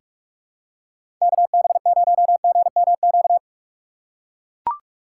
Morse code, heard as a keyed tone, about 44 wpm, 700 Hz. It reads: KB0GMQ
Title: USA Call Signs - 6 Characters - 6x Character Spacing - 45wpm